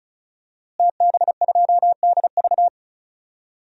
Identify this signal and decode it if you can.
Morse code, heard as a keyed tone, about 34 wpm, 700 Hz. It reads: TB2DV
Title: DXCC-100 Call Signs - 5 Characters - Repeated 2x - 35wpm